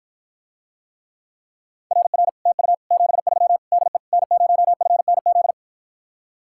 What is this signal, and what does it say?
Morse code, heard as a keyed tone, about 53 wpm, 700 Hz. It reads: RR TU 73 DE N0FNZ